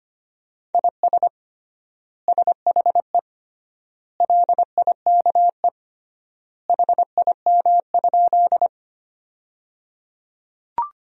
Morse code, heard as a keyed tone, about 25 wpm, 700 Hz. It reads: IS SHE LIKE HIM?
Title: 3x Word-Spacing - Sentences from Top 100 Words - 25wpm